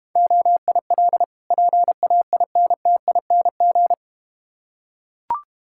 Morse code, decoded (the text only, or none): OIL PAINTING